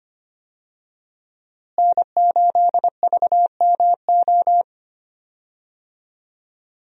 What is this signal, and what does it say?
Morse code, heard as a keyed tone, about 25 wpm, 700 Hz. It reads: N8VMO
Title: USA Call Signs - 5 Characters - Repeated 3x - 25wpm